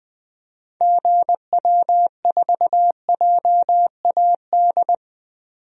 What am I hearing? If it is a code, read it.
Morse code, heard as a keyed tone, about 20 wpm, 700 Hz. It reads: GW4JAD
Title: DXCC-100 Call Signs - 6 Characters - 20wpm